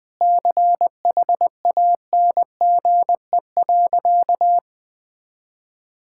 Morse code, heard as a keyed tone, about 20 wpm, 700 Hz. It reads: CHANGE.
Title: Sentences from Top 1000 Words - Review Part 5 - 20wpm